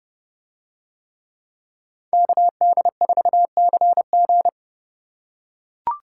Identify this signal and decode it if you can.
Morse code, heard as a keyed tone, about 30 wpm, 700 Hz. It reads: KD4CG